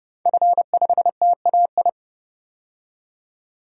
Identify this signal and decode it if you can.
Morse code, heard as a keyed tone, about 30 wpm, 700 Hz. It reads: F5TAI
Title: DXCC-100 Call Signs - 5 Characters - 30wpm